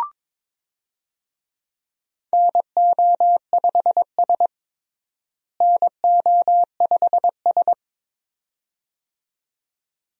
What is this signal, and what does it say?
Morse code, heard as a keyed tone, about 22 wpm, 700 Hz. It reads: NO5S NO5S
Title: USA Call Signs - 4 Characters - Repeated 2x - 22wpm